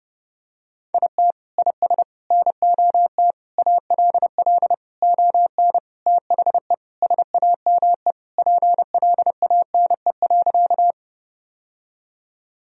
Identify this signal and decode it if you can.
Morse code, heard as a keyed tone, about 30 wpm, 700 Hz. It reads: IT IS NOT ALL ON THE SAME PLANE.
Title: Sentences from Top 1000 Words - Review Part 3 - 30wpm